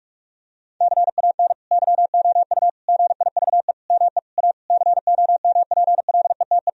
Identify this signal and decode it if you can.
Morse code, heard as a keyed tone, about 46 wpm, 700 Hz. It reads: CAN YOU GIVE ME A COMPLETE
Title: Sentences from Top 1000 Words - Review Part 2 - 45wpm